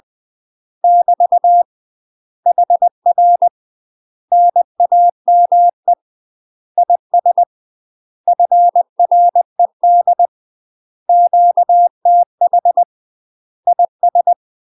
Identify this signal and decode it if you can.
Morse code, heard as a keyed tone, about 20 wpm, 700 Hz. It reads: <BT> HR NAME IS FRED QTH IS
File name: UK MASTER QSOs TRAINING / 20WPM / UK MASTER QSOs 0070